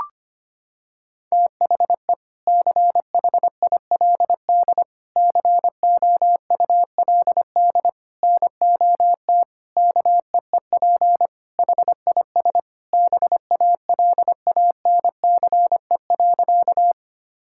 Morse code, heard as a keyed tone, about 25 wpm, 700 Hz. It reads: THE CHILD COULD NOT KEEP HIS BALANCE.
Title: Sentences from Top 1000 Words - Review Part 6 - 25wpm